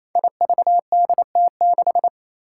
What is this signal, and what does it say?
Morse code, heard as a keyed tone, about 27 wpm, 700 Hz. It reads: IVDT6
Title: Five Challenging Letter-Number Combinations 28wpm